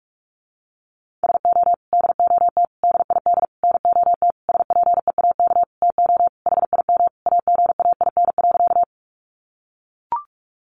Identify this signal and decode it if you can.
Morse code, heard as a keyed tone, about 44 wpm, 700 Hz. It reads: SO DOT DID NOT SPEAK TO HIM AGAIN.
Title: Sentences from Top 1500 Words 45wpm